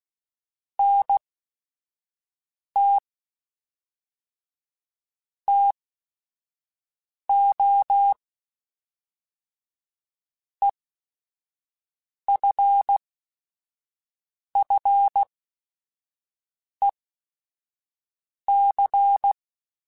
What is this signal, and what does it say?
Morse code, heard as a keyed tone, about 16 wpm, spaced as for 5 wpm, 800 Hz. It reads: NT TO EFFEC
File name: Technical Subjects / Technical Subjects-mp3s / XC @5